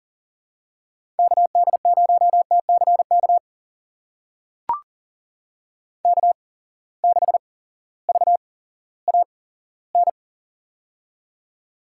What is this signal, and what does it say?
Morse code, heard as a keyed tone, about 40 wpm, spaced as for 18 wpm, 700 Hz. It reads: KD0TCK K6VAN
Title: USA Call Signs - 8x Character Spacing - 40wpm